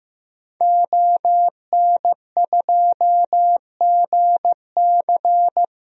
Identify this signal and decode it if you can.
Morse code, heard as a keyed tone, about 15 wpm, 700 Hz. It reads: ON2GC
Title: DXCC-100 Call Signs - 5 Characters - Repeated 3x - 15wpm